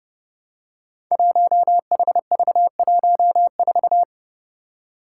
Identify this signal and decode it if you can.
Morse code, heard as a keyed tone, about 29 wpm, 700 Hz. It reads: 1HV14
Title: Five Challenging Letter-Number Combinations 30wpm